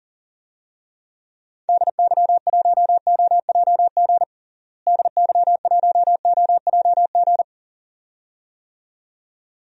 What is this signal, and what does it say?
Morse code, heard as a keyed tone, about 40 wpm, 700 Hz. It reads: DY1OJG DY1OJG
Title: DXCC-100 Call Signs - 6 Characters - Repeated 2x - 40wpm